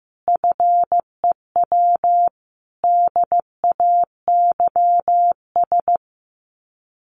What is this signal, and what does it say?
Morse code, heard as a keyed tone, about 15 wpm, 700 Hz. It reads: FEW DAYS
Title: Sets of 5 Words Encore2 15wpm